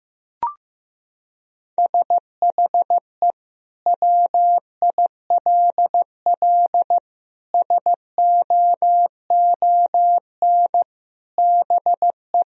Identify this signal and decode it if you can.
Morse code, heard as a keyed tone, about 16 wpm, 700 Hz. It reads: SHE WILL SOON BE
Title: Sentences from Top 1000 Words - Review Part 2 - 15wpm